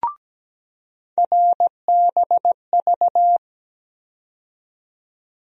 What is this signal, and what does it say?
Morse code, heard as a keyed tone, about 17 wpm, 700 Hz. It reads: RBV
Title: USA Call Sign Postfixes - 3 Letters 17wpm